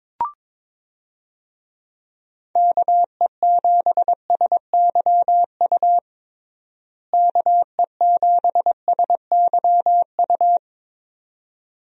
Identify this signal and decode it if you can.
Morse code, heard as a keyed tone, about 21 wpm, 700 Hz. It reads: KE7SYU KE7SYU
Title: USA Call Signs - 6 Characters - Repeated 2x - 22wpm